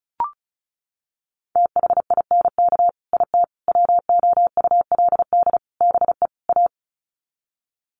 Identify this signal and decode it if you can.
Morse code, heard as a keyed tone, about 34 wpm, 700 Hz. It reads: THINK IT WOULD BE A
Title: Sets of 5 Words Encore2 35wpm